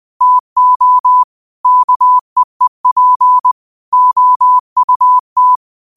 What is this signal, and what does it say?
Morse code, heard as a keyed tone, about 20 wpm, 1000 Hz. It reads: TO KEEP OUT
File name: qotdfiles / 20wpm